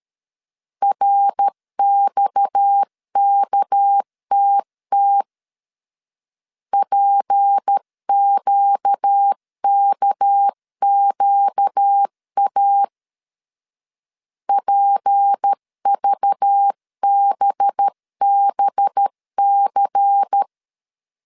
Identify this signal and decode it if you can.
Morse code, heard as a keyed tone, about 13 wpm, 800 Hz. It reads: RXKTT PQKQA PVBBC
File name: CW / DET / 1 6 1500C S C 12wpm 3533